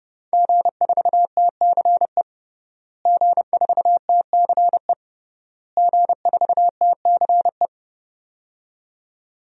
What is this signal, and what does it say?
Morse code, heard as a keyed tone, about 32 wpm, 700 Hz. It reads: G4TCE G4TCE G4TCE
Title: DXCC-100 Call Signs - 5 Characters - Repeated 3x - 30wpm